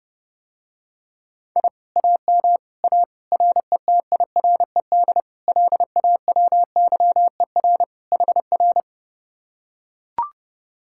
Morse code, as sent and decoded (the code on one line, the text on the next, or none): .. / .- -- / .- / .-. - .. .-. -.. / .-.. .- .-- -.-- . .-. / .... .-.
I AM A RETIRED LAWYER HR